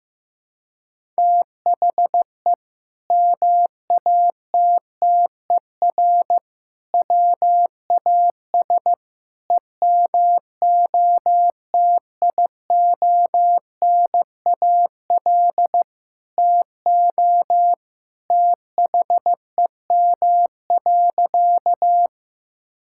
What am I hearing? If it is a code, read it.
Morse code, heard as a keyed tone, about 16 wpm, 700 Hz. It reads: THE MATTER WAS EMOTIONAL TO THEM.
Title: Sentences from Top 1000 Words - Review Part 5 - 15wpm